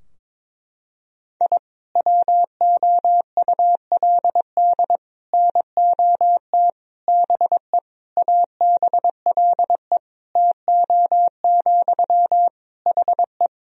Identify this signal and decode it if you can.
Morse code, heard as a keyed tone, about 21 wpm, 700 Hz. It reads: I WOULD NOT BE ABLE TO, HE
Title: Sentences from Top 1000 Words - Review Part 2 - 22wpm